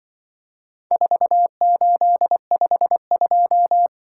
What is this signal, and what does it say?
Morse code, heard as a keyed tone, about 24 wpm, 700 Hz. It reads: 4852
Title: Speed-Racing - Four Letter-Number Combinations - 20wpm